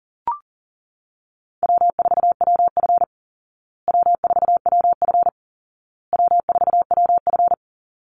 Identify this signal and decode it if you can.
Morse code, heard as a keyed tone, about 37 wpm, 700 Hz. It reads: W4WF W4WF W4WF
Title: Easy Call Signs Repeated 3x - 40wpm